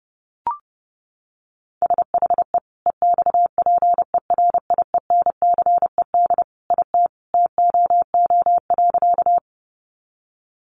Morse code, heard as a keyed tone, about 30 wpm, 700 Hz. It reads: SHE EXPERIENCED IT TOO.
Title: Sentences from Top 1100 Words 30wpm